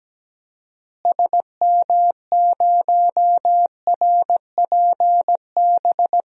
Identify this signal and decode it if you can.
Morse code, heard as a keyed tone, about 17 wpm, 700 Hz. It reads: SM0RPB
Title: DXCC-100 Call Signs - 6 Characters - Repeated 3x - 17wpm